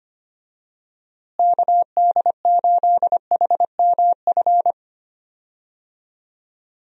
Morse code, heard as a keyed tone, about 25 wpm, 700 Hz. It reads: KD8HMF
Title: USA Call Signs - 6 Characters - Repeated 2x - 25wpm